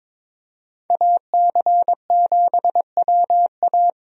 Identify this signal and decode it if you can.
Morse code, heard as a keyed tone, about 22 wpm, 700 Hz. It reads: AC7WA
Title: USA Call Signs - 5 Characters - Repeated 2x - 22wpm